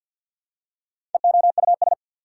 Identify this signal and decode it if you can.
Morse code, heard as a keyed tone, about 53 wpm, 690 Hz. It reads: EOUS
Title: Prefixes and Suffixes 50wpm